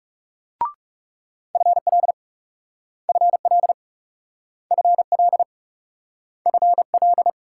Morse code, heard as a keyed tone, about 38 wpm, 700 Hz. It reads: FL FL FL FL